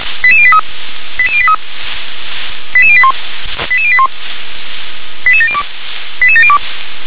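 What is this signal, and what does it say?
Five-tone selective call (ZVEI), heard as quick runs of five tones, tones of about 70 ms. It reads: 88083 88083 88081 88081 88702 88702
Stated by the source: ZVEI five-tone selective call received by SDR.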